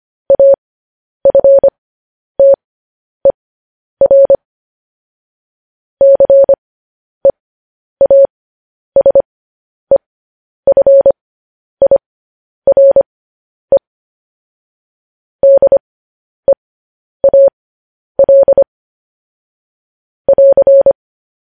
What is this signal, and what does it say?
Morse code, heard as a keyed tone, about 25 wpm, spaced as for 10 wpm, 550 Hz. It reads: AFTER CEASEFIRE DEAL <AR>
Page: News Headlines in Morse Code at 10 WPM